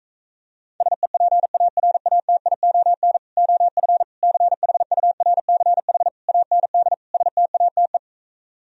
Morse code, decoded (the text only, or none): SEPARATION OF CHURCH AND STATE